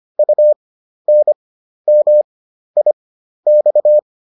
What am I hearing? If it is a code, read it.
Morse code, heard as a keyed tone, about 25 wpm, spaced as for 12 wpm, 600 Hz. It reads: UNMIX